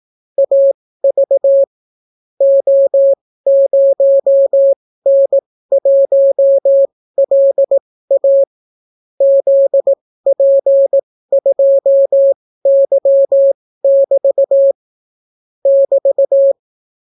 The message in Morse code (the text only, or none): AV O0N1LA ZP2Y<BT> <BT>